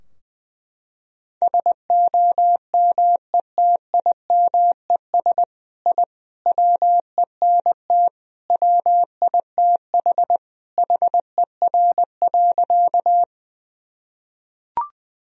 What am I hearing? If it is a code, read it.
Morse code, heard as a keyed tone, about 20 wpm, 700 Hz. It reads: SOMETIMES I WENT WITH HER.
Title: Sentences from Top 1000 Words - Review Part 3 - 20wpm